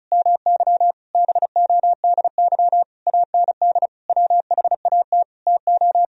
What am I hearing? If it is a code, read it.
Morse code, heard as a keyed tone, about 36 wpm, 700 Hz. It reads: MY BODY AND WHAT TO
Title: Sentences from Top 1000 Words - Review Part 2 - 35wpm